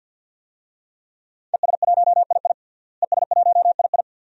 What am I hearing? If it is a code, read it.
Morse code, heard as a keyed tone, about 46 wpm, 700 Hz. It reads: ES1II ES1II
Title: DXCC-100 Call Signs - 5 Characters - Repeated 3x - 50wpm